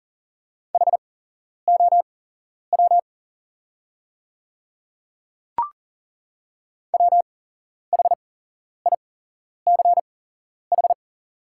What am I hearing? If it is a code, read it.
Morse code, heard as a keyed tone, about 40 wpm, spaced as for 11 wpm, 700 Hz. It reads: HOW WHICH
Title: 8x Character Spacing - Top 100 Words - 40wpm